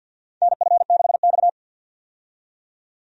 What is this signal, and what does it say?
Morse code, heard as a keyed tone, about 52 wpm, 700 Hz. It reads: NF6X
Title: USA Call Signs - 4 Characters - 50wpm